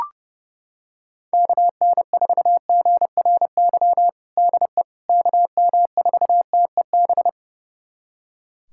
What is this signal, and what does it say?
Morse code, heard as a keyed tone, about 30 wpm, 700 Hz. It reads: KN4GRY DE KM4TEB
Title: USA Callsign DE Callsign 30wpm